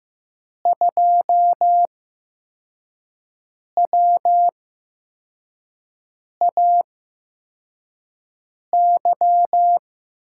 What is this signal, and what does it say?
Morse code, heard as a keyed tone, about 15 wpm, spaced as for 4 wpm, 700 Hz. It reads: 2WAY
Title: USA Call Signs - 8x Character Spacing - 15wpm